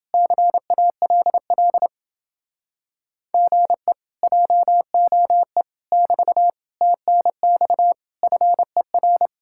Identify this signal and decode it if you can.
Morse code, heard as a keyed tone, about 30 wpm, 700 Hz. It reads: CALL GE JOE <BT> TNX FER